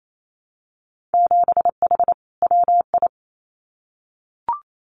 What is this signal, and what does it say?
Morse code, heard as a keyed tone, about 28 wpm, 700 Hz. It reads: 7H WI
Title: Winter Field Day Exchanges Encore 28wpm